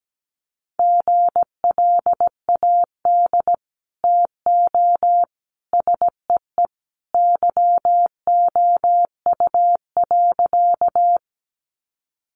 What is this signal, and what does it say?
Morse code, heard as a keyed tone, about 17 wpm, 700 Hz. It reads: GLAD TO SEE YOU.
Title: Sentences from Top 800 Words 17wpm